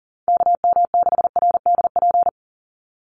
Morse code, heard as a keyed tone, about 40 wpm, 700 Hz. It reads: KM6RDP
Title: USA Call Signs 40wpm